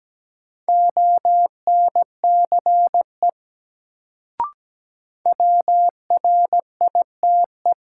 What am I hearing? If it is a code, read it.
Morse code, heard as a keyed tone, about 17 wpm, 700 Hz. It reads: ONCE WRITE
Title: Sentences from Top 300 Words - New Word Drill - 17wpm